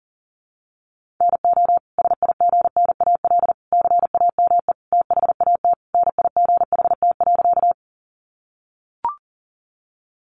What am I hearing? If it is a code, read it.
Morse code, heard as a keyed tone, about 40 wpm, 700 Hz. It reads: NO SIGNAL CAME THAT NIGHT.
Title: Sentences from Top 1400 Words 40wpm